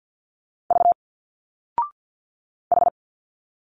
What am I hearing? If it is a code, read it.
Morse code, heard as a keyed tone, about 50 wpm, 700 Hz. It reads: V H
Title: Single Letters 50wpm